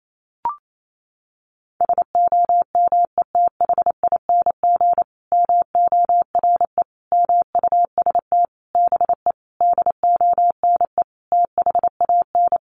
Morse code, heard as a keyed tone, about 28 wpm, 700 Hz. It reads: SOMETHING MORE MUST BE DONE THAN
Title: Sentences from Top 1400 Words 28wpm